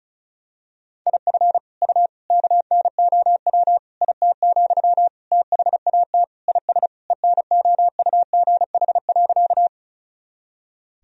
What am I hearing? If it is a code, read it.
Morse code, heard as a keyed tone, about 34 wpm, 700 Hz. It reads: IF U KNOW IT, THAT IS ENOUGH.
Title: Sentences from Top 400 Words and Abbreviations 35wpm